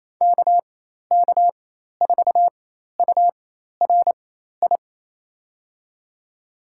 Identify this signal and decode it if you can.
Morse code, heard as a keyed tone, about 28 wpm, spaced as for 13 wpm, 700 Hz. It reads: KK4URI
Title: USA Call Signs - 4x Character Spacing - 28wpm